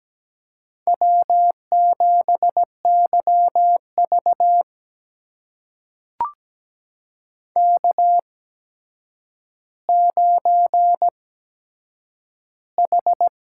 Morse code, heard as a keyed tone, about 17 wpm, spaced as for 8 wpm, 700 Hz. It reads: W7YV K9H
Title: USA Call Signs - 8x Character Spacing - 17wpm